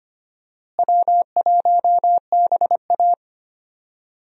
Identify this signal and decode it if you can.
Morse code, heard as a keyed tone, about 25 wpm, 700 Hz. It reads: W1BA
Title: USA Call Signs - 4 Characters - Repeated 2x - 25wpm